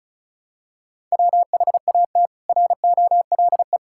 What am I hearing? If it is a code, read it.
Morse code, heard as a keyed tone, about 34 wpm, 700 Hz. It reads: WHAT ROLE
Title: Sentences from Top 1000 Words - Review Part 6 - 35wpm